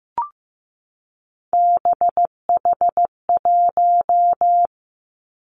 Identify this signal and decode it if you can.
Morse code, heard as a keyed tone, about 15 wpm, 700 Hz. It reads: BH1
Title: Three Challenging Letter-Number Combinations 15wpm